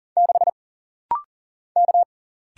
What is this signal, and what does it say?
Morse code, heard as a keyed tone, about 40 wpm, 700 Hz. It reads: 6 K
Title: Single Letter-Number - Rapid-Fire 40wpm